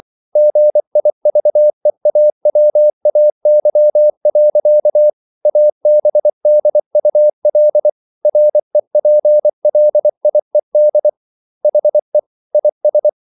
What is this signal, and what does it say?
Morse code, heard as a keyed tone, about 24 wpm, 600 Hz. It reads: GIVEAWAY. ABDUL REPLIED HE IS